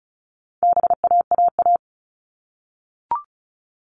Morse code, heard as a keyed tone, about 34 wpm, 700 Hz. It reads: BAAA
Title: All Cut Numbers - 4-Digit 35wpm